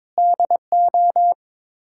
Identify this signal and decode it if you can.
Morse code, heard as a keyed tone, about 21 wpm, 700 Hz. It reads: DO
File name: practice / 22wpm / Build A Phrase - 5-Word Phrases - 22wpm